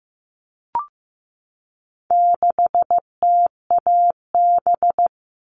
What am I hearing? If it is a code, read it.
Morse code, heard as a keyed tone, about 15 wpm, 700 Hz. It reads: 6TAB
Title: All Cut Numbers - 4-Digit 15wpm